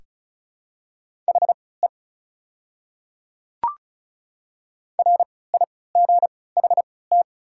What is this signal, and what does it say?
Morse code, heard as a keyed tone, about 34 wpm, spaced as for 20 wpm, 700 Hz. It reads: HE RIGHT
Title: 3x Character Spacing - Top 300 Words - 35wpm